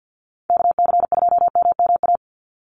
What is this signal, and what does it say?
Morse code, heard as a keyed tone, about 50 wpm, 700 Hz. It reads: KC2MMA
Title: USA Call Signs 50wpm